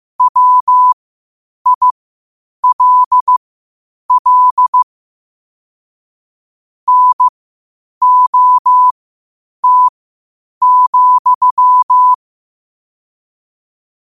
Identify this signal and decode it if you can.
Morse code, heard as a keyed tone, about 15 wpm, spaced as for 8 wpm, 1000 Hz. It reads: WILL NOT,
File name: qotdfiles / 8wpm